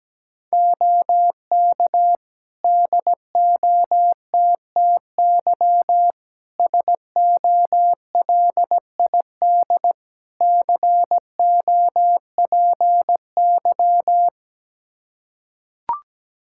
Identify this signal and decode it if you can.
Morse code, heard as a keyed tone, about 17 wpm, 700 Hz. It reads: OK DOTTY SOLID COPY
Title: K1USN SST QSO Practice 17wpm